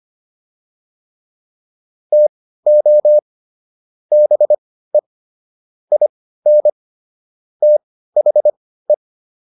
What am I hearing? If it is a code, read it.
Morse code, heard as a keyed tone, about 25 wpm, spaced as for 15 wpm, 600 Hz. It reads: TO BE IN THE
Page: intermediate-practice-files